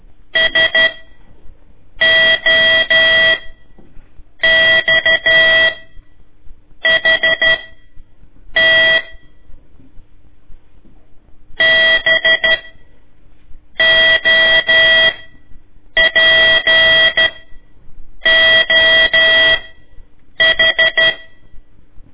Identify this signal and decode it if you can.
Morse code, heard as a keyed tone, about 12 wpm, spaced as for 6 wpm, 1950 Hz. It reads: SOXHT BOPOH
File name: ik1weg2 / CORSI CW / lez4-1